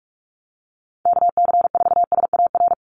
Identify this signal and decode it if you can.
Morse code, heard as a keyed tone, about 44 wpm, 700 Hz. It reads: KC4SAR
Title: USA Call Signs 45wpm